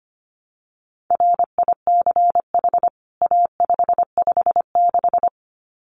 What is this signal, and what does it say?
Morse code, heard as a keyed tone, about 25 wpm, 700 Hz. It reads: RICH A556
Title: CWT Exchange 25wpm